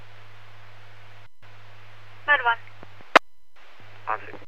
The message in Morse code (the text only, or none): none